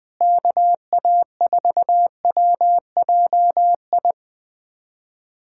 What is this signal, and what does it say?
Morse code, heard as a keyed tone, about 20 wpm, 700 Hz. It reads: KA4WJI
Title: USA Callsign DE Callsign - 6 Characters - 20wpm